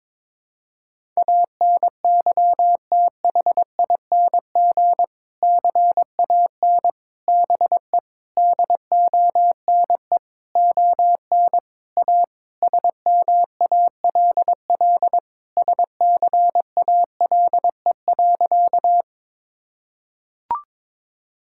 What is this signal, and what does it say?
Morse code, heard as a keyed tone, about 22 wpm, 700 Hz. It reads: ANYTHING CAN BE DONE ON A SMALL SCALE.
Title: Sentences from Top 1000 Words - Review Part 5 - 22wpm